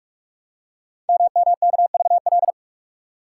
Morse code, heard as a keyed tone, about 44 wpm, 690 Hz. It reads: MMKVL
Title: Code Groups - 5 Letters - 45wpm